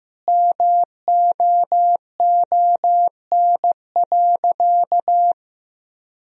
Morse code, heard as a keyed tone, about 15 wpm, 700 Hz. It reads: MOON.
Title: Sentences from Top 1000 Words - Review Part 3 - 15wpm